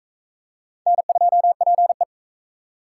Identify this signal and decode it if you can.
Morse code, heard as a keyed tone, about 38 wpm, 700 Hz. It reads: N2PE